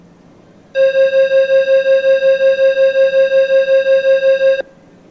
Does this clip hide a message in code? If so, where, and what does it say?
No.